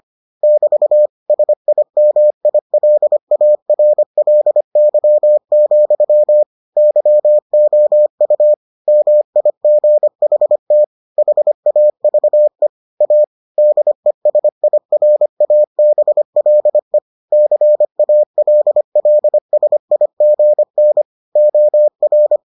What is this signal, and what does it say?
Morse code, heard as a keyed tone, about 25 wpm, 600 Hz. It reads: <BT> SIMILARLY, YOU MIGHT HAVE A DESIRABLE CALLSIGN OR